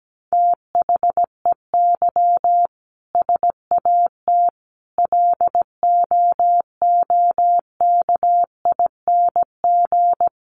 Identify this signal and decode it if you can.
Morse code, heard as a keyed tone, about 17 wpm, 700 Hz. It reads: THEY SAT LOOKING
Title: A Sound Of Thunder 17wpm